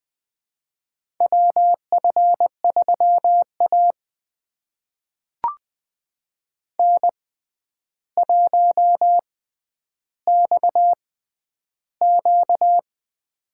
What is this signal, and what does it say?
Morse code, heard as a keyed tone, about 20 wpm, spaced as for 10 wpm, 700 Hz. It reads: WF3A N1XQ
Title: USA Call Signs - 4 Characters - 6x Character Spacing - 20wpm